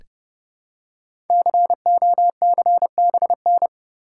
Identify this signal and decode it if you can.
Morse code, heard as a keyed tone, about 30 wpm, 700 Hz. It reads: COCBN
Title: Code Groups - 5 Letters - 30wpm